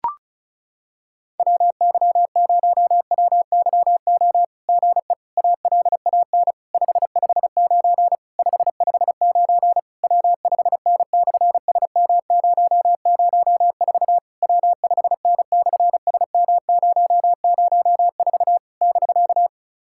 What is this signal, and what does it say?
Morse code, heard as a keyed tone, about 36 wpm, 700 Hz. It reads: WY0WYO GE ALAN 559 559 W5N/SM004 W5N/SM004 <BK>
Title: SOTA QSO Practice 35wpm